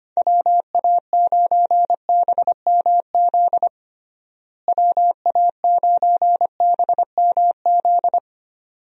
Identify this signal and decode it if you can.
Morse code, heard as a keyed tone, about 25 wpm, 700 Hz. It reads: WA9BMZ WA9BMZ
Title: USA Call Signs - 6 Characters - Repeated 2x - 25wpm